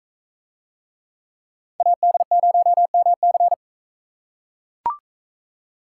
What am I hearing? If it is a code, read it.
Morse code, heard as a keyed tone, about 40 wpm, 700 Hz. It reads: AD0MC